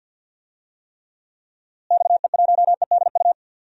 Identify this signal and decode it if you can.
Morse code, heard as a keyed tone, about 47 wpm, 700 Hz. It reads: XE1EDU
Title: DXCC-100 Call Signs - 6 Characters - Repeated 3x - 50wpm